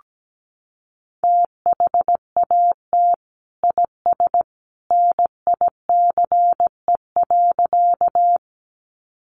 Morse code, heard as a keyed tone, about 17 wpm, 700 Hz. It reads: THAT IS NICE.
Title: Sentences from Top 1100 Words 17wpm